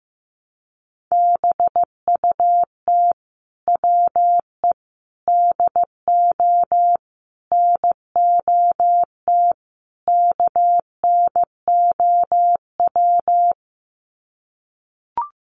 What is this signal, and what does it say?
Morse code, heard as a keyed tone, about 15 wpm, 700 Hz. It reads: BUT WE DO NOT KNOW
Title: Sets of 5 Words Encore2 15wpm